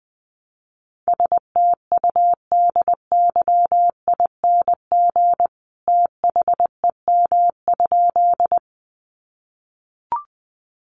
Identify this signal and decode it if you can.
Morse code, heard as a keyed tone, about 20 wpm, 700 Hz. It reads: STUDYING THEM?
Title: A Sound Of Thunder 20wpm